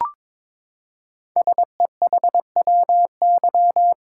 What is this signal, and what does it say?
Morse code, heard as a keyed tone, about 22 wpm, 700 Hz. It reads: SEHWY
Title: Code Groups - 5 Letters - 22wpm